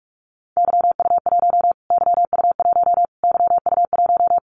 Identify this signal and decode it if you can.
Morse code, heard as a keyed tone, about 44 wpm, 700 Hz. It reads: YU1 YU1 YU1
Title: Callsign Prefixes for Top 25 DXCC - Repeated 3x - 45wpm